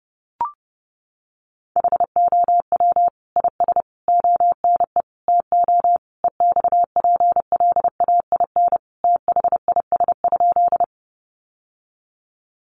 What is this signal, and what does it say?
Morse code, heard as a keyed tone, about 30 wpm, 700 Hz. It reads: HOW IS ONE TO EXPLAIN THIS?
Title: Sentences from Top 1100 Words 30wpm